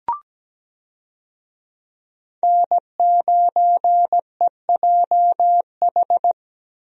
Morse code, heard as a keyed tone, about 17 wpm, 700 Hz. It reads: N9EJH
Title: USA Call Signs - 5 Characters - Repeated 3x - 17wpm